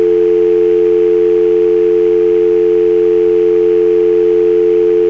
Dial tone (North America), 350 + 440 Hz.
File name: Dial Tone